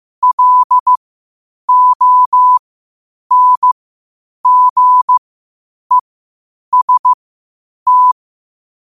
Morse code, heard as a keyed tone, about 15 wpm, spaced as for 8 wpm, 1000 Hz. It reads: LONGEST
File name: qotdfiles / 8wpm